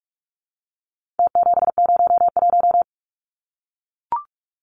Morse code, heard as a keyed tone, about 44 wpm, 700 Hz. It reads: T701
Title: Numbers - Contest Style - 4-Digit 45wpm